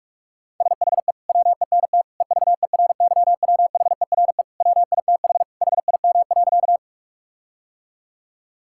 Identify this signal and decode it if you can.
Morse code, heard as a keyed tone, about 46 wpm, 700 Hz. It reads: SHE WENT EVERYWHERE WITH HIM.
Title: Sentences from Top 1600 Words 45wpm